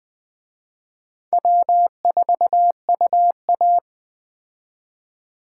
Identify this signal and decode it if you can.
Morse code, heard as a keyed tone, about 21 wpm, 700 Hz. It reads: W4UA
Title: Speed Racing - USA 1x2 Call Signs 17wpm